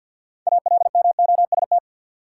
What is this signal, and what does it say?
Morse code, heard as a keyed tone, about 47 wpm, 700 Hz. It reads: ALMOST
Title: Sentences from Top 700 Words - New Word Drill - 50wpm